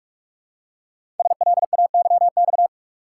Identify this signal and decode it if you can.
Morse code, heard as a keyed tone, about 44 wpm, 700 Hz. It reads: SLAYX
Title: Code Groups - 5 Letters - 45wpm